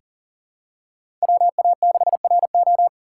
Morse code, heard as a keyed tone, about 40 wpm, 700 Hz. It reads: WA6RO
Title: USA Call Signs - 3x Character Spacing - 40wpm